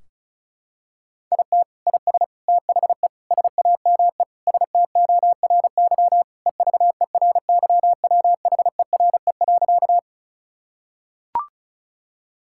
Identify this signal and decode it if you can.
Morse code, heard as a keyed tone, about 36 wpm, 700 Hz. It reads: IT IS THE SAME STORY EVERYWHERE.
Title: Sentences from Top 1600 Words 35wpm